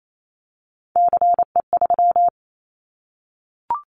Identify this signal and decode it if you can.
Morse code, heard as a keyed tone, about 28 wpm, 700 Hz. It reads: CE3
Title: Callsign Prefixes for Top 100 DXCC - 28wpm